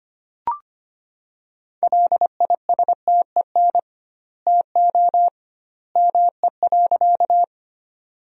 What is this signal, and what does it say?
Morse code, heard as a keyed tone, about 25 wpm, 700 Hz. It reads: LISTEN TO ME.
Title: 2x Word-Spacing - Sentences from Top 400 Words - 25wpm